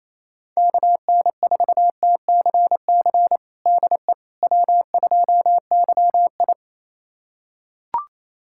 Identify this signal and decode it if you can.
Morse code, heard as a keyed tone, about 28 wpm, 700 Hz. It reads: KN4TCC DE W2YI
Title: USA Callsign DE Callsign 28wpm